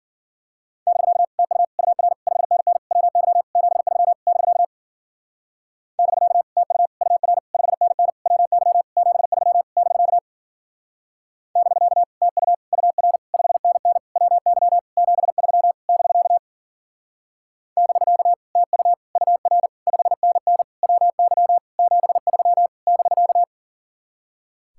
Morse code, heard as a keyed tone, about 49 wpm, 700 Hz. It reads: <BK> TU UR 5NN WY 73 <BK> <BK> TU UR 5NN WY 73 <BK> <BK> TU UR 5NN WY 73 <BK> <BK> TU UR 5NN WY 73 <BK>
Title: Speed-Racing Common QSO Phrases - 40wpm